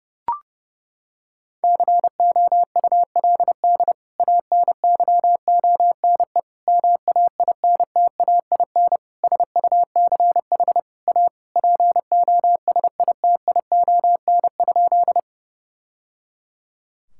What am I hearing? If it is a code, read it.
Morse code, heard as a keyed tone, about 30 wpm, 700 Hz. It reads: COULD ANYONE MAINTAIN SUCH A POSITION?
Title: Sentences from Top 1000 Words - Review Part 5 - 30wpm